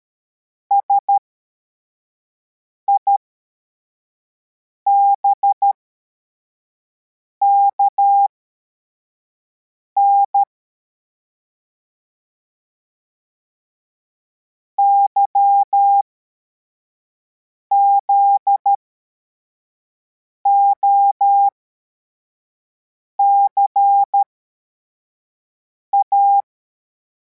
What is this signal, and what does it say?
Morse code, heard as a keyed tone, about 13 wpm, spaced as for 4 wpm, 800 Hz. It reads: SIBKN YZOCA
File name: formation / cw / Text63